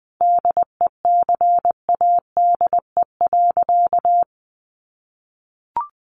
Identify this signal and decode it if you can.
Morse code, heard as a keyed tone, about 19 wpm, 700 Hz. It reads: DECADE.
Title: A Sound Of Thunder 20wpm